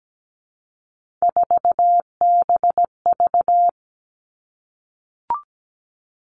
Morse code, heard as a keyed tone, about 17 wpm, 700 Hz. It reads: 4BV
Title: Three Challenging Letter-Number Combinations 17wpm